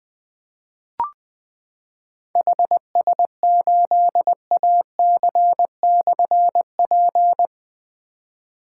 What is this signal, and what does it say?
Morse code, heard as a keyed tone, about 20 wpm, 700 Hz. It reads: HS8AC/P
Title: Difficult Call Signs 20wpm